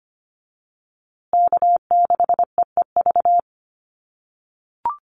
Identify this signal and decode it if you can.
Morse code, heard as a keyed tone, about 25 wpm, 700 Hz. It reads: K6EEV
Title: USA Call Signs Repeated 2x 25wpm